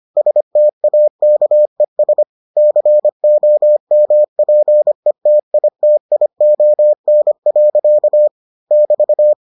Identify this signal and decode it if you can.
Morse code, heard as a keyed tone, about 25 wpm, 600 Hz. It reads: STAKES COMPETITION. <BT>